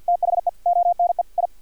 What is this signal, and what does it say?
Morse code, heard as a keyed tone, about 50 wpm, 700 Hz. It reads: THE ONE I
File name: archived / CRSnet